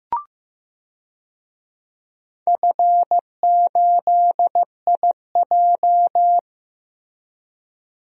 Morse code, heard as a keyed tone, about 15 wpm, 700 Hz. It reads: F8IJ
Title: DXCC-100 Call Signs - 4 Characters - Repeated 3x - 15wpm